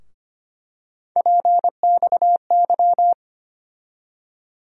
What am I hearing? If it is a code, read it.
Morse code, heard as a keyed tone, about 25 wpm, 700 Hz. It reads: PXY
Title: USA Call Sign Postfixes - 3 Letters 25wpm